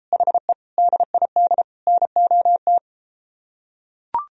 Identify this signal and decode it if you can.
Morse code, heard as a keyed tone, about 33 wpm, 700 Hz. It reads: HE DID NOT